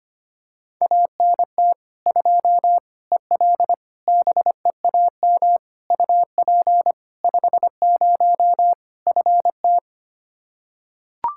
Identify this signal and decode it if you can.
Morse code, heard as a keyed tone, about 25 wpm, 700 Hz. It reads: ANT 2 EL BEAM UP 50 FT
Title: Practice QSO - Exchange 2 - Info - 25wpm